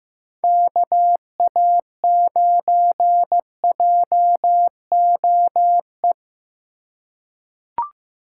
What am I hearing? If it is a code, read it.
Morse code, heard as a keyed tone, about 16 wpm, 700 Hz. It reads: KA9JOE
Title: USA Call Signs - 4x Character Spacing - 15wpm